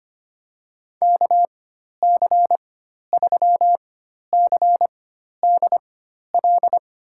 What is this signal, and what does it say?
Morse code, heard as a keyed tone, about 25 wpm, spaced as for 12 wpm, 700 Hz. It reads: KC3CDL
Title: USA Call Signs - 6 Characters - 4x Character Spacing - 25wpm